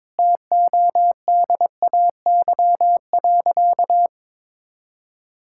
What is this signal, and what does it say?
Morse code, heard as a keyed tone, about 22 wpm, 700 Hz. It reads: TODAY.
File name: practice / 22wpm / Sentences with 3-Word Phrases - 22wpm